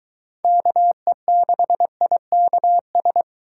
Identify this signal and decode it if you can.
Morse code, heard as a keyed tone, about 24 wpm, 700 Hz. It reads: KE6IKS